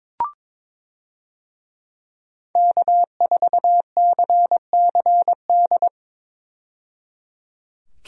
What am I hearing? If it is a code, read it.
Morse code, heard as a keyed tone, about 22 wpm, 700 Hz. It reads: K4CCD
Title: USA Call Signs - 5 Characters - 22wpm